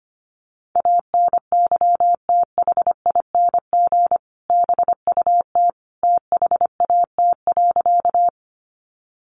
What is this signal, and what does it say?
Morse code, heard as a keyed tone, about 25 wpm, 700 Hz. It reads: ANYTHING BUT THAT.
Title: Sentences from Top 1000 Words 25wpm